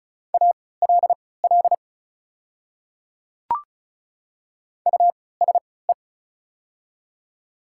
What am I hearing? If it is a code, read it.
Morse code, heard as a keyed tone, about 36 wpm, spaced as for 20 wpm, 700 Hz. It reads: ALL USE
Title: 3x Character Spacing - Top 300 Words - 35wpm